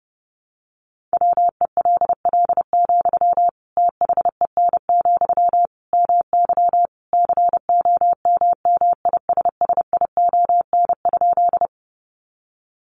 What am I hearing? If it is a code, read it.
Morse code, heard as a keyed tone, about 30 wpm, 700 Hz. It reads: WELL, THEN, MY COMMISSION?
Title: Sentences from Top 1400 Words 30wpm